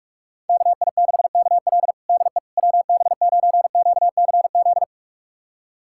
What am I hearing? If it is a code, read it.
Morse code, heard as a keyed tone, about 44 wpm, 700 Hz. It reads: KI6KL DE WB9QCZ
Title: USA Callsign DE Callsign 45wpm